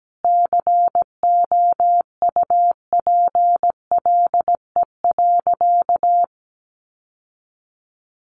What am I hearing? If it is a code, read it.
Morse code, heard as a keyed tone, about 17 wpm, 700 Hz. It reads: COUPLE.
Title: Sentences from Top 1200 Words 17wpm